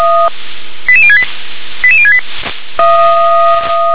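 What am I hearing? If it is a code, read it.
Five-tone selective call (ZVEI), heard as quick runs of five tones, tones of about 70 ms. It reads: 88068 88068
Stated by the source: ZVEI five-tone selective call 88119, received by SDR.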